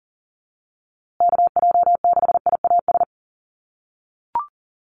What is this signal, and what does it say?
Morse code, heard as a keyed tone, about 40 wpm, 700 Hz. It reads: KJ6IAS
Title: USA Call Signs Repeated 2x 40wpm